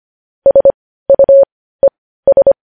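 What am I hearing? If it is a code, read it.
Morse code, heard as a keyed tone, about 25 wpm, spaced as for 15 wpm, 550 Hz. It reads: SUES
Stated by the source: Morse code news headlines sent at 15 wpm.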